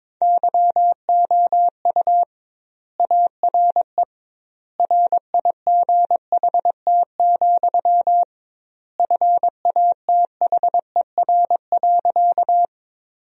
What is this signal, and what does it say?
Morse code, heard as a keyed tone, about 22 wpm, 700 Hz. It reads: YOU ARE RIGHT, FATHER.
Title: 2x Word-Spacing - Sentences from Top 300 Words - 22wpm